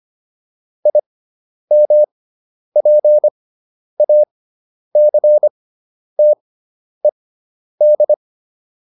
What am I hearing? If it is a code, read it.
Morse code, heard as a keyed tone, about 25 wpm, spaced as for 10 wpm, 600 Hz. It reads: IMPACTED